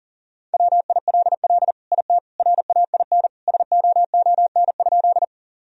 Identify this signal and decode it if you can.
Morse code, heard as a keyed tone, about 40 wpm, 700 Hz. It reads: WILL IT RAIN SOON?
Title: Sentences from Top 1000 Words - Review Part 1 - 40wpm